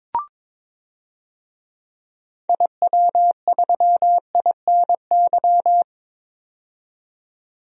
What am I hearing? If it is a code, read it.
Morse code, heard as a keyed tone, about 21 wpm, 700 Hz. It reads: IW3INY
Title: DXCC-100 Call Signs - 6 Characters - 22wpm